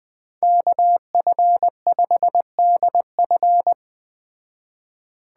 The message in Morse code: KF5DF

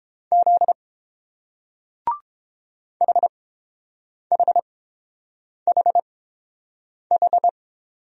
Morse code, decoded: Z HHHH

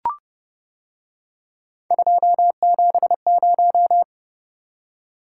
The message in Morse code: 270

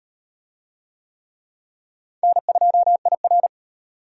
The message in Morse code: N2IR

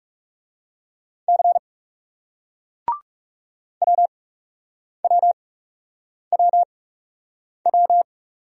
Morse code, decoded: C W W W W

